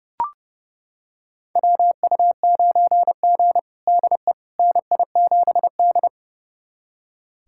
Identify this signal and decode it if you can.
Morse code, heard as a keyed tone, about 30 wpm, 700 Hz. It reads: WU9G DE NI7D
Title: USA Callsign DE Callsign - 4 Characters - 30wpm